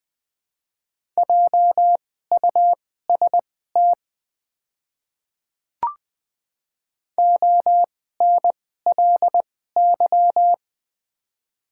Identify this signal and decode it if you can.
Morse code, heard as a keyed tone, about 20 wpm, spaced as for 14 wpm, 700 Hz. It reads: JUST ONLY
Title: 2x Character Spacing - Top 300 Words - 20wpm